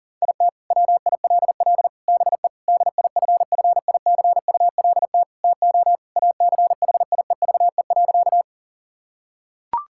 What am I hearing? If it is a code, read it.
Morse code, heard as a keyed tone, about 40 wpm, 700 Hz. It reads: IT WILL BE DIFFICULT TO ACHIEVE.
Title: Sentences from Top 1000 Words - Review Part 6 - 40wpm